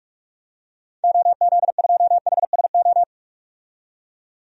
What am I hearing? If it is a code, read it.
Morse code, heard as a keyed tone, about 42 wpm, 700 Hz. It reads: OZ2HSO
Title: DXCC-100 Call Signs - 6 Characters - 45wpm